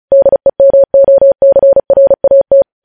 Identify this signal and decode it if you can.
Morse code, heard as a keyed tone, about 36 wpm, 550 Hz. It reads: DEMOCRAT
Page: News Headlines in Morse Code at 35 WPM